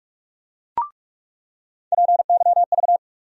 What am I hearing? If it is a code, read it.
Morse code, heard as a keyed tone, about 47 wpm, 700 Hz. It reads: PYV
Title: USA Call Sign Postfixes - 3 Letters 45wpm